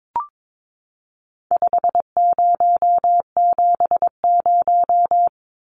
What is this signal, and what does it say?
Morse code, heard as a keyed tone, about 22 wpm, 700 Hz. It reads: 5070
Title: Numbers - 4-Digit 22wpm